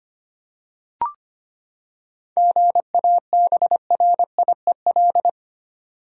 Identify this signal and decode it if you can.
Morse code, heard as a keyed tone, about 25 wpm, 700 Hz. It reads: GABRIEL
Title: USA Common Names - 25wpm